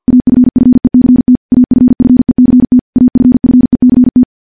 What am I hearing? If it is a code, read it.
Morse code, heard as a keyed tone, about 50 wpm, 250 Hz. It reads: AFFECT AFFECT AFFECT
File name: morse / morse-training-50WPM